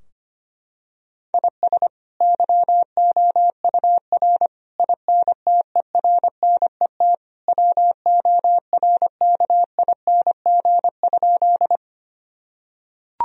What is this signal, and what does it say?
Morse code, heard as a keyed tone, about 25 wpm, 700 Hz. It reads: IS YOUR INTERNET WORKING?
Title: Sentences from Top 1000 Words - Review Part 4 - 25wpm